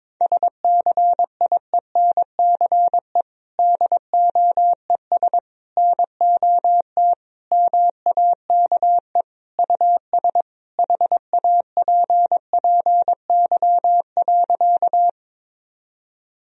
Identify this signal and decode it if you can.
Morse code, heard as a keyed tone, about 22 wpm, 700 Hz. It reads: SCIENCE DOES NOT MAKE US HAPPY.
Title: Sentences from Top 1000 Words - Review Part 2 - 22wpm